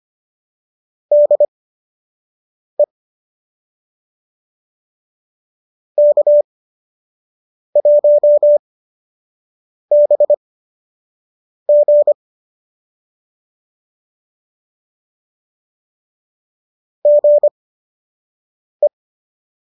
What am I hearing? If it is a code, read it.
Morse code, heard as a keyed tone, about 25 wpm, spaced as for 6 wpm, 590 Hz. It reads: DE K1BG GE